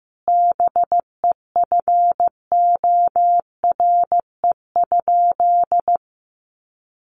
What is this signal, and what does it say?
Morse code, heard as a keyed tone, about 15 wpm, 700 Hz. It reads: BEFORE?
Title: Sentences from Top 1200 Words 15wpm